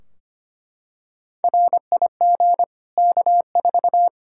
Morse code, heard as a keyed tone, about 25 wpm, 700 Hz. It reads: RIG K4
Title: Practice QSO - Exchange 2 - Info - 25wpm